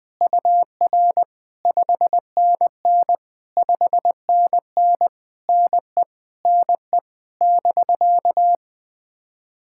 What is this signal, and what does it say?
Morse code, heard as a keyed tone, about 20 wpm, 700 Hz. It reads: UR 5NN 5NN NE NE <BK>
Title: POTA QSO Practice - 20wpm